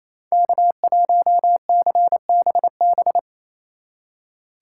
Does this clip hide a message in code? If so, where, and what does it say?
Yes: Morse code from 0.3 to 3.2 s, heard as a keyed tone, about 37 wpm, 700 Hz. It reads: K1CBB